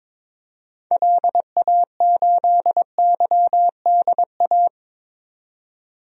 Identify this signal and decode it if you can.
Morse code, heard as a keyed tone, about 21 wpm, 700 Hz. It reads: LA8YDA
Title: DXCC-100 Call Signs - 6 Characters - 22wpm